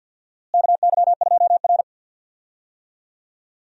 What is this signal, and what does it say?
Morse code, heard as a keyed tone, about 46 wpm, 700 Hz. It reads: KY2R
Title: USA Call Signs - 4 Characters - Repeated 3x - 50wpm